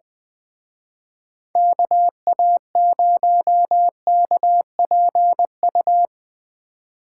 Morse code, heard as a keyed tone, about 22 wpm, 700 Hz. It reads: KA0KPU
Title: Speed Racing - USA 2x3 Call Signs 20wpm